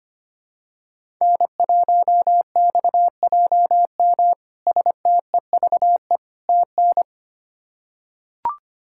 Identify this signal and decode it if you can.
Morse code, heard as a keyed tone, about 25 wpm, 700 Hz. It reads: N1XJM STEVE TN
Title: K1USN SST QSO Practice - 25wpm